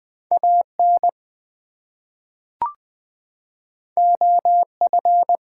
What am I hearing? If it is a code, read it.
Morse code, heard as a keyed tone, about 20 wpm, 700 Hz. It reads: AN OF
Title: Two Letter Words - 20wpm